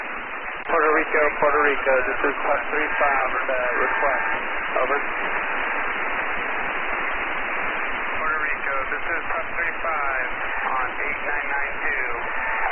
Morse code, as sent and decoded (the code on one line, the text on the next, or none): none